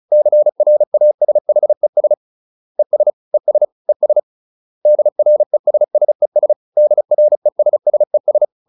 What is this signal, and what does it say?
Morse code, heard as a keyed tone, about 34 wpm, 600 Hz. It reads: CRASHES ES ES ES DRESSES DRESSES